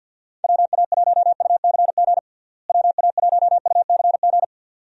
Morse code, heard as a keyed tone, about 52 wpm, 700 Hz. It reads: WA1UCG WA1UCG
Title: USA Call Signs - 6 Characters - Repeated 2x - 50wpm